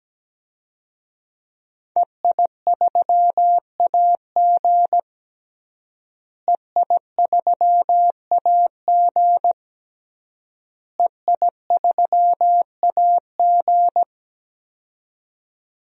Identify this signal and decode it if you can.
Morse code, heard as a keyed tone, about 17 wpm, 700 Hz. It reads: EI3AG EI3AG EI3AG
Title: DXCC-100 Call Signs - 5 Characters - Repeated 3x - 17wpm